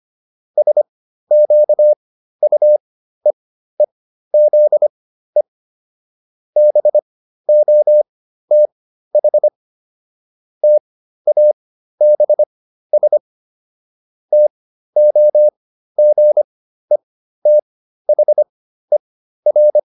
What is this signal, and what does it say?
Morse code, heard as a keyed tone, about 25 wpm, spaced as for 13 wpm, 600 Hz. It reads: SQUEEZE BOTH TABS TOGETHER